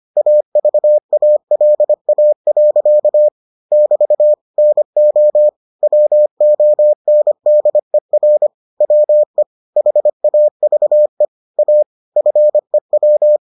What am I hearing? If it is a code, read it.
Morse code, heard as a keyed tone, about 25 wpm, 600 Hz. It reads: AVALA. <BT> NO WONDER WE HAVE A FEW